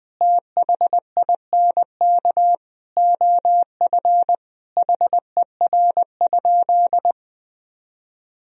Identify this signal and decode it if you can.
Morse code, heard as a keyed tone, about 20 wpm, 700 Hz. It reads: THINK OF HER?
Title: Sentences from Top 1000 Words - Review Part 4 - 20wpm